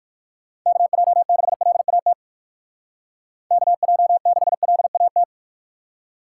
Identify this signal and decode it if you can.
Morse code, heard as a keyed tone, about 57 wpm, 700 Hz. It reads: KJ6LAT KJ6LAT